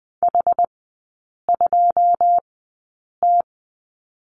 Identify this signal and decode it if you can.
Morse code, heard as a keyed tone, about 20 wpm, spaced as for 8 wpm, 690 Hz. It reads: H2T
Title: 081 Lesson 14e - Number 2 - Three Characters 20wpm